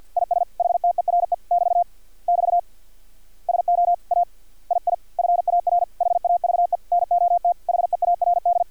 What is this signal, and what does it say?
Morse code, heard as a keyed tone, about 50 wpm, 700 Hz. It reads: IS HTERE <BT> <BT> SO A II FAR HAVE NOT HEARD
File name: archived / CRSnet